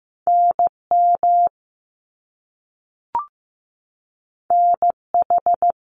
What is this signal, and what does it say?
Morse code, heard as a keyed tone, about 15 wpm, 700 Hz. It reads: NM NH
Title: US States 15wpm